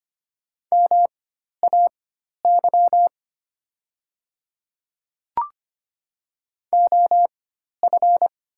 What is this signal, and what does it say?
Morse code, heard as a keyed tone, about 25 wpm, spaced as for 12 wpm, 700 Hz. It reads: MAY OF
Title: 4x Character Spacing - Top 100 Words - 25wpm